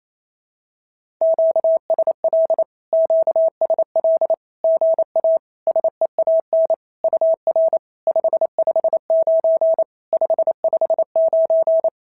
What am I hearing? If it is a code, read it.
Morse code, heard as a keyed tone, about 28 wpm, 650 Hz. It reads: QSL QSL GA SEAN UR 559 559
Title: SOTA QSO Practice 28wpm